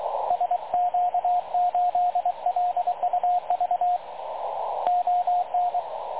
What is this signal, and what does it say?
Morse code, heard as a keyed tone, about 23 wpm, 720 Hz, with radio noise. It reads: SQ8LUV ON
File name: dlazaawansowanych / nagraniazpasma / websdr recording 3567.0kHz